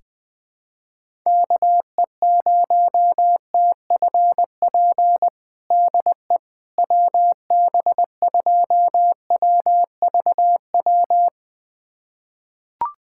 Morse code, heard as a keyed tone, about 20 wpm, 700 Hz. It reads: KE0TFP DE WB2WVW
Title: USA Callsign DE Callsign - 6 Characters - 20wpm